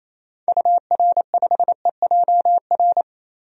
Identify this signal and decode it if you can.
Morse code, heard as a keyed tone, about 28 wpm, 700 Hz. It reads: UR5EJR
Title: DXCC-100 Call Signs - 6 Characters - 28wpm